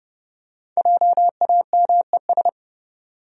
Morse code, heard as a keyed tone, about 31 wpm, 700 Hz. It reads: JAMES
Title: USA Common Names - 30wpm